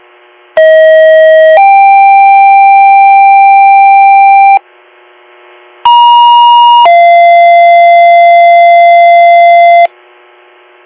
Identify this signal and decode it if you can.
Two-tone page, heard as a one-second tone, then a three-second tone, about 640 then 790 Hz, about 950 then 690 Hz. It is heard from 0.6 to 9.9 s.